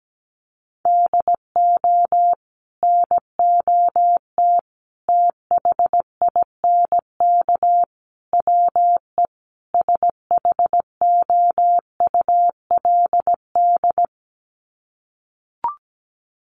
Morse code, heard as a keyed tone, about 17 wpm, 700 Hz. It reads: DO NOT THINK WE SHOULD
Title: Sets of 5 Words Encore 17wpm